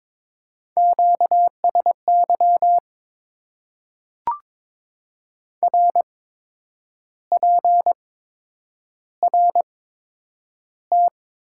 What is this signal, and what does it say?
Morse code, heard as a keyed tone, about 22 wpm, spaced as for 8 wpm, 700 Hz. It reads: QSY RPRT
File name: Top 90 QSO Elements - 8x Character Spacing - 22wpm